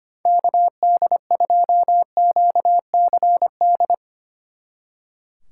Morse code, heard as a keyed tone, about 32 wpm, 700 Hz. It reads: KD2QCD